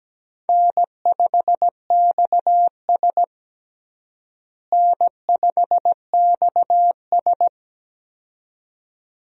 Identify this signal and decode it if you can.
Morse code, heard as a keyed tone, about 17 wpm, 700 Hz. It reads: N5XS N5XS
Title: USA Call Signs - 4 Characters - Repeated 2x - 17wpm